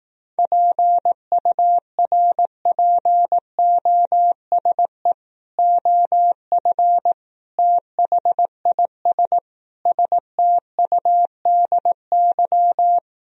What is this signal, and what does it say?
Morse code, heard as a keyed tone, about 18 wpm, 700 Hz. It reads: PURPOSE OF THIS STUDY
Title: Speed-Racing - Sets of 5 Words 15wpm